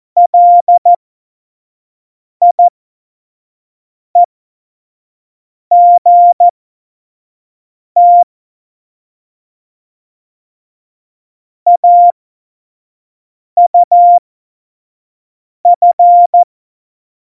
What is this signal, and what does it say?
Morse code, heard as a keyed tone, about 14 wpm, spaced as for 5 wpm, 700 Hz. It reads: LIEGT AUF
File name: Morsekurs / Witze / Witz 23. Januar 2018, 25 F70